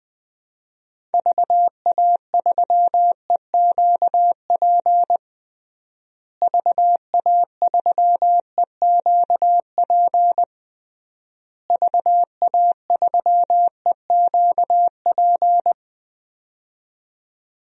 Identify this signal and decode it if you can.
Morse code, heard as a keyed tone, about 21 wpm, 700 Hz. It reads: VA3EQP VA3EQP VA3EQP
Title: DXCC-100 Call Signs - 6 Characters - Repeated 3x - 20wpm